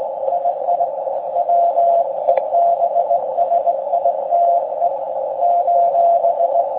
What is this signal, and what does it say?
Morse code, heard as a keyed tone, about 19 wpm, 690 Hz, with noise. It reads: II WEBSITE :
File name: forum / im / aq grimeton